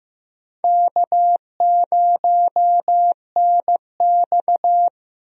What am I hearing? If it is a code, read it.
Morse code, heard as a keyed tone, about 15 wpm, 700 Hz. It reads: K0NX
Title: USA Call Signs - 4 Characters - Repeated 2x - 15wpm